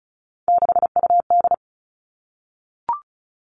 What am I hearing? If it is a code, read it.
Morse code, heard as a keyed tone, about 36 wpm, 700 Hz. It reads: 6UD
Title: Three Challenging Letter-Number Combinations 35wpm